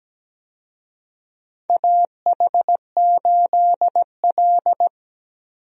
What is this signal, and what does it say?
Morse code, heard as a keyed tone, about 17 wpm, 700 Hz. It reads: AH8L
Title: USA Call Signs - 4 Characters - 17wpm